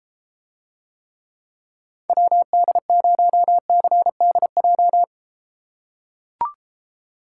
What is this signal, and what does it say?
Morse code, heard as a keyed tone, about 33 wpm, 700 Hz. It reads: WD0CDJ